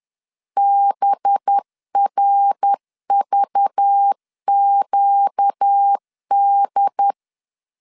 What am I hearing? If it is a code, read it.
Morse code, heard as a keyed tone, about 10 wpm, 800 Hz. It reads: BRVQD